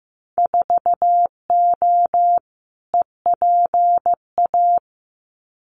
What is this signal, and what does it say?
Morse code, heard as a keyed tone, about 15 wpm, 700 Hz. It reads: 4O EPA
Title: Winter Field Day Exchanges 15wpm